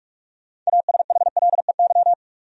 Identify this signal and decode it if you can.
Morse code, heard as a keyed tone, about 45 wpm, 700 Hz. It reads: ASHLEY
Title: USA Common Names - 45wpm